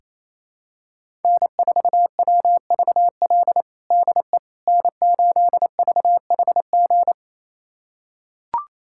Morse code, heard as a keyed tone, about 28 wpm, 700 Hz. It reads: N4WVL DE N8VHG
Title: USA Callsign DE Callsign - 5 Characters - 28wpm